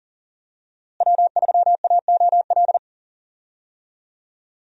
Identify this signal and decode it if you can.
Morse code, heard as a keyed tone, about 40 wpm, 700 Hz. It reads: W3AOL
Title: USA Call Signs - 5 Characters - Repeated 2x - 40wpm